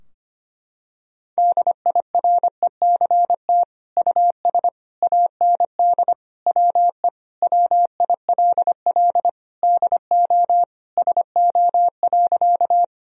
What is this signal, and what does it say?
Morse code, heard as a keyed tone, about 25 wpm, 700 Hz. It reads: DIRECT US AND WE WILL DO SO.
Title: Sentences from Top 1000 Words - Review Part 2 - 25wpm